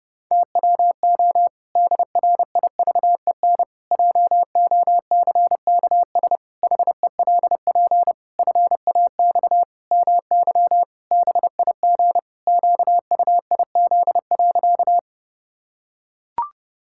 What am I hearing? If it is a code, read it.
Morse code, heard as a keyed tone, about 32 wpm, 700 Hz. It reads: TWO DRIVEN JOCKS HELP FAX MY BIG QUIZ.
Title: Pangram Challenge v2 - 30wpm